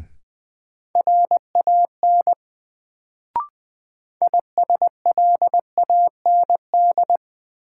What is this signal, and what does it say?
Morse code, heard as a keyed tone, about 20 wpm, 700 Hz. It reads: RAN ISLAND
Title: Sentences from Top 500 Words - New Word Drill - 20wpm